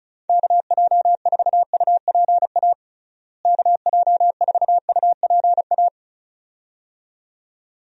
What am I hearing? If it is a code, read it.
Morse code, heard as a keyed tone, about 34 wpm, 700 Hz. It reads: KJ4UPA KJ4UPA
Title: USA Call Signs - 6 Characters - Repeated 2x - 35wpm